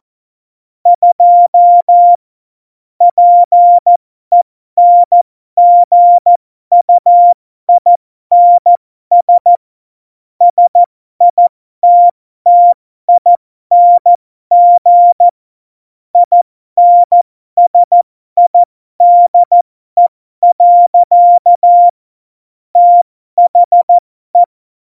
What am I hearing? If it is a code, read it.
Morse code, heard as a keyed tone, about 14 wpm, spaced as for 12 wpm, 700 Hz. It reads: 2 PENGUINS SITTING INSIDE. THE
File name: Morsekurs / Witze